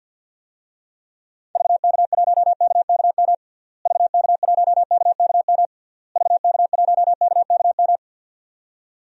Morse code, heard as a keyed tone, about 50 wpm, 700 Hz. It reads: VK1KKM VK1KKM VK1KKM
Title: DXCC-100 Call Signs - 6 Characters - Repeated 3x - 50wpm